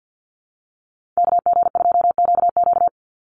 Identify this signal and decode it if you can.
Morse code, heard as a keyed tone, about 49 wpm, 700 Hz. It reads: KG2QQ
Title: USA Call Signs 50wpm